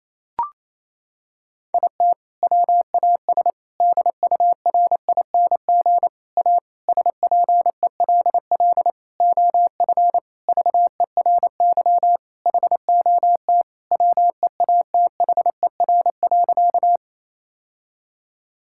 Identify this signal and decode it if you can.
Morse code, heard as a keyed tone, about 28 wpm, 700 Hz. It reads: IT WAS DURING A SPELL OF VERY HOT WEATHER.
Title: Sentences from Top 1000 Words - Review Part 5 - 28wpm